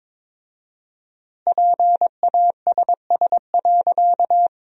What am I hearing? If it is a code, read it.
Morse code, heard as a keyed tone, about 22 wpm, 690 Hz. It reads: PASS.
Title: 4x Word-Spacing - Sentences from Top 400 Words - 22wpm